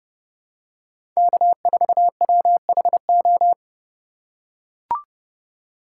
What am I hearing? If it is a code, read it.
Morse code, heard as a keyed tone, about 30 wpm, 700 Hz. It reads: K4WHO
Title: USA Call Signs - 5 Characters - 2x Character Spacing - 30wpm